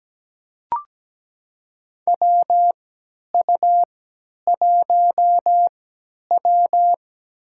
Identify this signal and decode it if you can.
Morse code, heard as a keyed tone, about 17 wpm, spaced as for 10 wpm, 700 Hz. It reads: WU1W
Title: USA Call Signs - 4 Characters - 3x Character Spacing - 17wpm